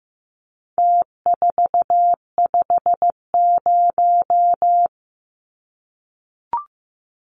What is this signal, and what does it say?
Morse code, heard as a keyed tone, about 15 wpm, 700 Hz. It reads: T450
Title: Numbers - Contest Style - 4-Digit 15wpm